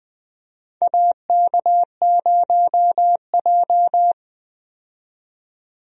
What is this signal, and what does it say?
Morse code, heard as a keyed tone, about 21 wpm, 700 Hz. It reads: AK0J
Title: USA Call Signs - 4 Characters - 20wpm